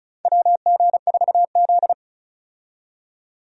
Morse code, heard as a keyed tone, about 36 wpm, 700 Hz. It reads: WG4Z
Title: USA Call Signs - 4 Characters - 35wpm